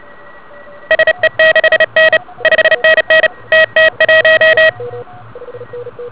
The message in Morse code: SE6N 5NN TT1